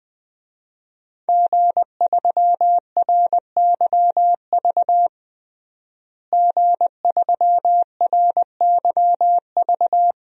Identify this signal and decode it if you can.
Morse code, heard as a keyed tone, about 20 wpm, 700 Hz. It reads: G3RYV G3RYV
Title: DXCC-100 Call Signs - 5 Characters - Repeated 2x - 20wpm